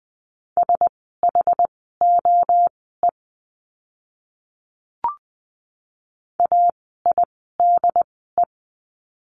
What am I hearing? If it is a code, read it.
Morse code, heard as a keyed tone, about 20 wpm, spaced as for 14 wpm, 700 Hz. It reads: SHOE AIDE
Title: 068 Lesson 12f - Letter d - Words 20wpm